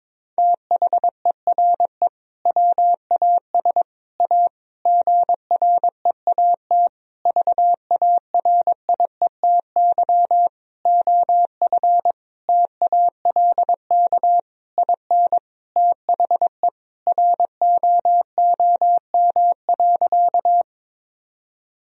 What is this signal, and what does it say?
Morse code, heard as a keyed tone, about 22 wpm, 700 Hz. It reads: THERE WAS A GREAT VARIETY OF TALK IN THE ROOM.
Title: Sentences from Top 1000 Words - Review Part 4 - 22wpm